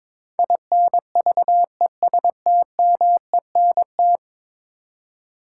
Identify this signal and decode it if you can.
Morse code, heard as a keyed tone, about 22 wpm, 700 Hz. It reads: INVESTMENT
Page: morse-code-ninja-practice-22wpm